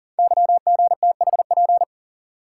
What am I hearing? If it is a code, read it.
Morse code, heard as a keyed tone, about 40 wpm, 700 Hz. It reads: YGTHP
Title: Code Groups - 5 Letters - 40wpm